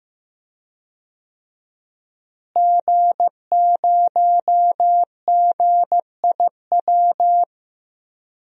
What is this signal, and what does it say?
Morse code, heard as a keyed tone, about 16 wpm, 700 Hz. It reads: G0GIW
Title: DXCC-100 Call Signs - 5 Characters - Repeated 3x - 15wpm